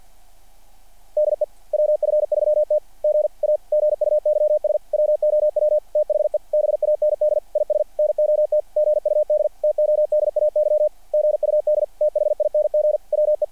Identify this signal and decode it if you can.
Morse code, heard as a keyed tone, about 50 wpm, 600 Hz. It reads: DE WF3T G A GUYS WOW THE BAND IS NOT GUD TODAY GUD THING WE
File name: archived / CRSnet